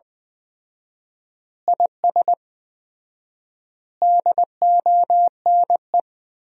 Morse code, heard as a keyed tone, about 20 wpm, 700 Hz. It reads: IS DONE
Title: 4x Word-Spacing - Sentences from Top 500 Words - 20wpm